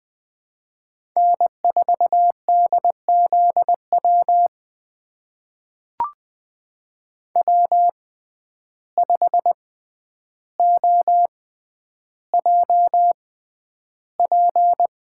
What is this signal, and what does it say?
Morse code, heard as a keyed tone, about 21 wpm, spaced as for 10 wpm, 700 Hz. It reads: N4DZW W5OJP
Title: USA Call Signs - 5 Characters - 6x Character Spacing - 20wpm